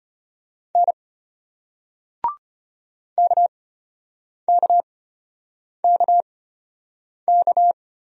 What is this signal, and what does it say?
Morse code, heard as a keyed tone, about 34 wpm, spaced as for 8 wpm, 700 Hz. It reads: N KKKK